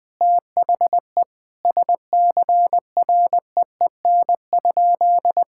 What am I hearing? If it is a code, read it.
Morse code, heard as a keyed tone, about 20 wpm, 700 Hz. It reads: THE SCREEN?
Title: Sentences from Top 1000 Words - Review Part 6 - 20wpm